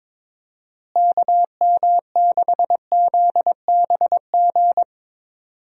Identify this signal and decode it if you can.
Morse code, heard as a keyed tone, about 22 wpm, 700 Hz. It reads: KM6ZBG
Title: USA Call Signs - 6 Characters - Repeated 2x - 22wpm